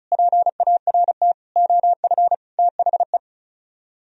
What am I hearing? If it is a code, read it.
Morse code, heard as a keyed tone, about 44 wpm, 700 Hz. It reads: PART OF THE
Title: Speed-Racing - Sets of 4 Words 35wpm